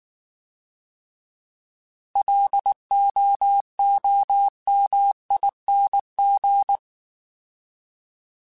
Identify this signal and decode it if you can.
Morse code, heard as a keyed tone, about 19 wpm, 800 Hz. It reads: LOOMING
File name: Alphabetical / Alphabetical mp3s / Ls @18